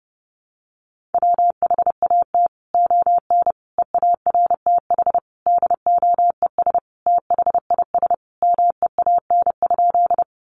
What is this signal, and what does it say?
Morse code, heard as a keyed tone, about 30 wpm, 700 Hz. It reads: WHAT ON EARTH DOES THIS MEAN?
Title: Sentences from Top 200 Words 30wpm